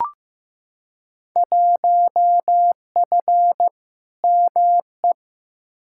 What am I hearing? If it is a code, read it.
Morse code, heard as a keyed tone, about 15 wpm, 700 Hz. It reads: 1F ME
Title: Field Day Exchanges Encore - 15wpm